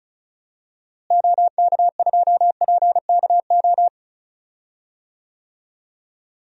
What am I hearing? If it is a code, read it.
Morse code, heard as a keyed tone, about 34 wpm, 700 Hz. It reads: OK2PKO
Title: DXCC-100 Call Signs - 6 Characters - Repeated 2x - 35wpm